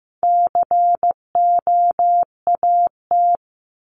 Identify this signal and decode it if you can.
Morse code, heard as a keyed tone, about 15 wpm, 700 Hz. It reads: COAT
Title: Top 1501-1750 Words 15wpm